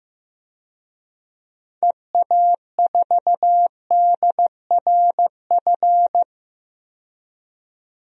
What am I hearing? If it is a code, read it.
Morse code, heard as a keyed tone, about 15 wpm, 700 Hz. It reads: EA4DRF
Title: DXCC-100 Call Signs - 6 Characters - 15wpm